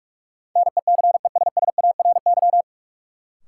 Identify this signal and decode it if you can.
Morse code, heard as a keyed tone, about 44 wpm, 700 Hz. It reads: NECESSARY
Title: Sentences from Top 800 Words - New Word Drill - 45wpm